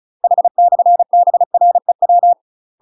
Morse code, heard as a keyed tone, about 34 wpm, 700 Hz. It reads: H/BREW
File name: UK MASTER QSOs TRAINING/35WPM